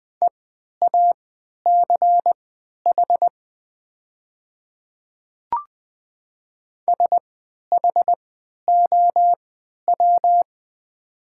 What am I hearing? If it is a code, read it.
Morse code, heard as a keyed tone, about 20 wpm, spaced as for 11 wpm, 690 Hz. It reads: EACH SHOW
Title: 3x Character Spacing - Top 200 Words - 20wpm